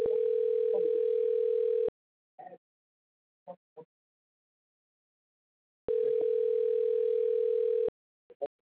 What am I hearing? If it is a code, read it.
Ringback tone (North America), 440 + 480 Hz.